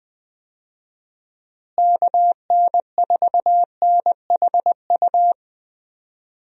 Morse code, heard as a keyed tone, about 20 wpm, 700 Hz. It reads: KN4NHU
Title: USA Call Signs - 6 Characters - Repeated 2x - 20wpm